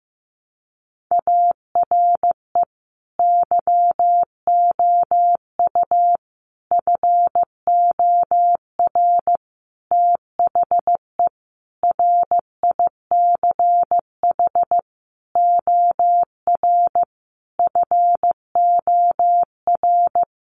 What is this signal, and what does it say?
Morse code, heard as a keyed tone, about 15 wpm, 700 Hz. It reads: ARE YOU FOR THE RICH OR FOR
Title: Sentences from Top 1100 Words 15wpm